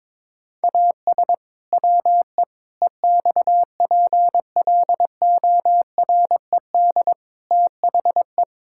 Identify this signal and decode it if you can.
Morse code, heard as a keyed tone, about 22 wpm, 700 Hz. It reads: AS WE EXPLORED THE
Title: Pangram Challenge v2 - 22wpm